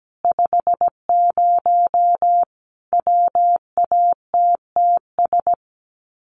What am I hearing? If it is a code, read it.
Morse code, heard as a keyed tone, about 17 wpm, 700 Hz. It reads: 50 WATTS
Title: Common QSO Phrases 17wpm